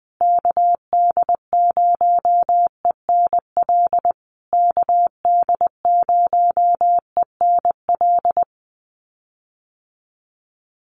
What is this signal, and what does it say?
Morse code, heard as a keyed tone, about 20 wpm, 700 Hz. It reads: KD0ENL KD0ENL
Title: USA Call Signs Repeated 3x 20wpm